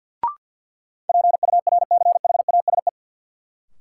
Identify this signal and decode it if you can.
Morse code, heard as a keyed tone, about 51 wpm, 700 Hz. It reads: PURCHASE
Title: Sentences from Top 800 Words - New Word Drill - 50wpm